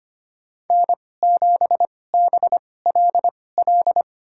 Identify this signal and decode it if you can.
Morse code, heard as a keyed tone, about 25 wpm, spaced as for 18 wpm, 700 Hz. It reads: N7BLL
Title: USA Call Signs - 5 Characters - 2x Character Spacing - 25wpm